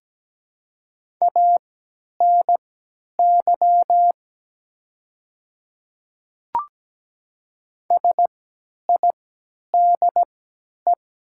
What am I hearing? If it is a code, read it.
Morse code, heard as a keyed tone, about 17 wpm, spaced as for 10 wpm, 700 Hz. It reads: ANY SIDE
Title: 3x Character Spacing - Top 300 Words - 17wpm